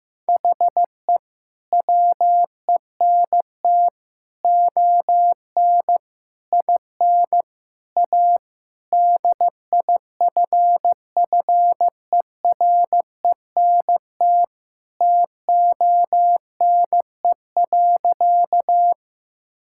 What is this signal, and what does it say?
Morse code, heard as a keyed tone, about 15 wpm, 700 Hz. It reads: HE WENT ON IN A DIFFERENT TONE.
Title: Sentences from Top 1600 Words 15wpm